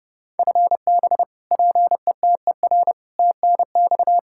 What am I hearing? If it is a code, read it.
Morse code, heard as a keyed tone, about 30 wpm, 700 Hz. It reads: FB PETER TNX
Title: Practice QSO - Exchange 3 - More Info - 30wpm